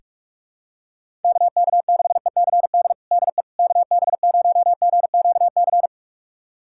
Morse code, heard as a keyed tone, about 46 wpm, 700 Hz. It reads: KK6ECD DE KB0GQC
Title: USA Callsign DE Callsign - 6 Characters - 45wpm